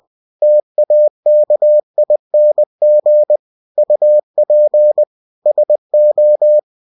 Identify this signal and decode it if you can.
Morse code, heard as a keyed tone, about 20 wpm, 600 Hz. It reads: TAKING UP SO
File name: cw-books / winnie-the-pooh / winnie-20wpm-page-0091